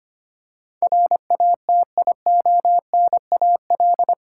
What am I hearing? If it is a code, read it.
Morse code, heard as a keyed tone, about 25 wpm, 690 Hz. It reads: RATIONAL
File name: 8 Letter Words - First 3 Letters Given - 25wpm